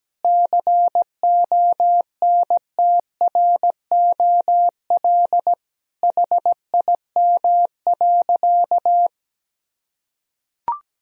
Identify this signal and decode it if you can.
Morse code, heard as a keyed tone, about 17 wpm, 700 Hz. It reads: CONTROL HIM.
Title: Sentences from Top 1000 Words - Review Part 3 - 17wpm